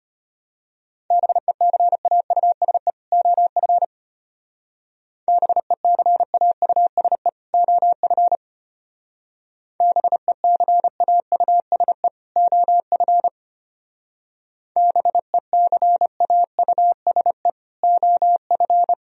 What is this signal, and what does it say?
Morse code, heard as a keyed tone, about 32 wpm, 700 Hz. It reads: BECAUSE OF BECAUSE OF BECAUSE OF BECAUSE OF